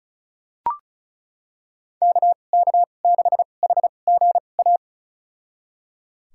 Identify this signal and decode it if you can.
Morse code, heard as a keyed tone, about 34 wpm, spaced as for 25 wpm, 700 Hz. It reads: KK6HGA
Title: USA Call Signs - 6 Characters - 2x Character Spacing - 35wpm